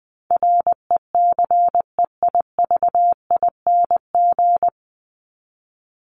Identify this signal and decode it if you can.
Morse code, heard as a keyed tone, about 20 wpm, 700 Hz. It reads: RECEIVING
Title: ING Suffix - Top 100 Words with Base Words 20wpm